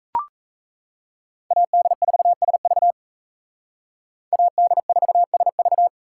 Morse code, heard as a keyed tone, about 36 wpm, 700 Hz. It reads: AD4SV AD4SV